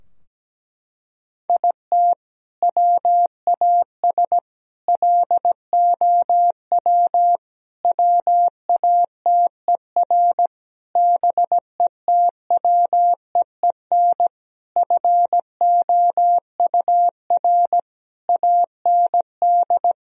Morse code, heard as a keyed tone, about 17 wpm, 700 Hz. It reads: IT WAS LOW WATER BETWEEN FOUR AND
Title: Sentences from Top 1000 Words - Review Part 2 - 17wpm